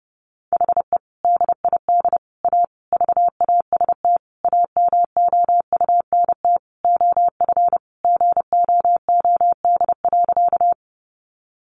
Morse code, heard as a keyed tone, about 30 wpm, 700 Hz. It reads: HE DID A VAST AMOUNT OF GOOD.
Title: Sentences from Top 1300 Words 30wpm